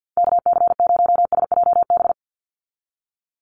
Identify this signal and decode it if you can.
Morse code, heard as a keyed tone, about 50 wpm, 700 Hz. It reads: KC0SJB
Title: USA Call Signs 50wpm